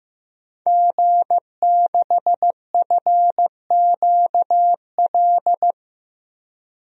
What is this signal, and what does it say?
Morse code, heard as a keyed tone, about 15 wpm, 700 Hz. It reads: G6FQL
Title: DXCC-100 Call Signs - 5 Characters - Repeated 2x - 15wpm